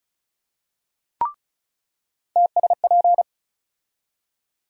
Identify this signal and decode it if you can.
Morse code, heard as a keyed tone, about 36 wpm, 700 Hz. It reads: TSP
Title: USA Call Sign Postfixes - 3 Letters 35wpm